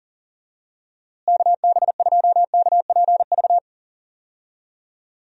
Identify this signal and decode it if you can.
Morse code, heard as a keyed tone, about 40 wpm, 700 Hz. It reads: KB2KPV
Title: USA Call Signs - 6 Characters - Repeated 3x - 40wpm